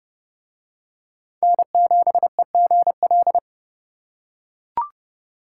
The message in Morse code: N7EGL

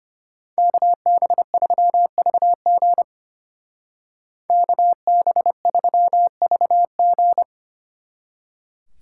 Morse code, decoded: KB3VG KB3VG

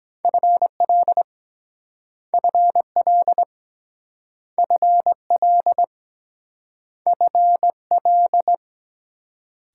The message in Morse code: FL FL FL FL

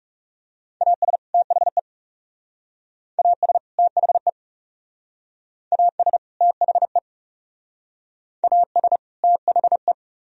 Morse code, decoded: AS THE AS THE AS THE AS THE

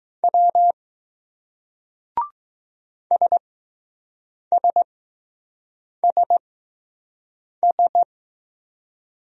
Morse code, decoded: W SSSS

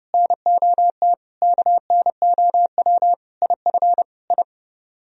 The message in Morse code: NOT KNOW IF I

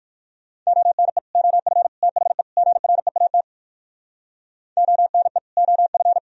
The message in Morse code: ONE OF THE GREAT ONE OF

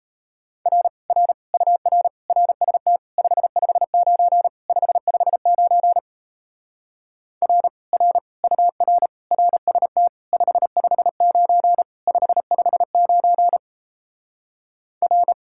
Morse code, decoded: R R UR RST 559 559 R R UR RST 559 559 R

